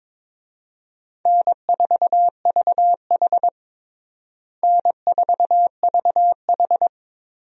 Morse code, heard as a keyed tone, about 22 wpm, 700 Hz. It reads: N4VH N4VH
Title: USA Call Signs - 4 Characters - Repeated 3x - 22wpm